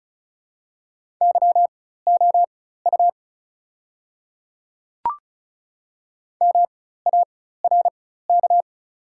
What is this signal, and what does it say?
Morse code, heard as a keyed tone, about 36 wpm, spaced as for 16 wpm, 700 Hz. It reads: YOU MARK
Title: 4x Character Spacing - Top 500 Words - 35wpm